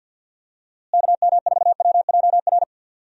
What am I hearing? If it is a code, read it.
Morse code, heard as a keyed tone, about 52 wpm, 700 Hz. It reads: KM4WJR
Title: USA Call Signs - 6 Characters - Repeated 2x - 50wpm